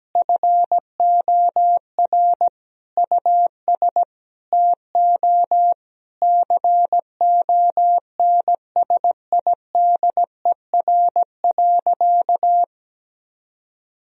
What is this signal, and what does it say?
Morse code, heard as a keyed tone, about 17 wpm, 700 Hz. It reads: FOR US TO CONSIDER.
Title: Sentences from Top 1000 Words - Review Part 4 - 17wpm